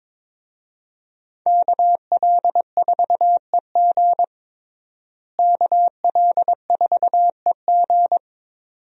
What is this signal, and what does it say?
Morse code, heard as a keyed tone, about 22 wpm, 700 Hz. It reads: KL4EG KL4EG
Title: USA Call Signs - 5 Characters - Repeated 2x - 22wpm